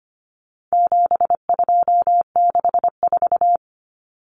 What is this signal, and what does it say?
Morse code, heard as a keyed tone, about 25 wpm, 700 Hz. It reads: 7264
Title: Numbers - 4-Digit 25wpm